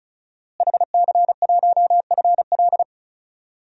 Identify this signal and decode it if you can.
Morse code, heard as a keyed tone, about 34 wpm, 700 Hz. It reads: HC1FL
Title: DXCC-100 Call Signs - 5 Characters - Repeated 2x - 35wpm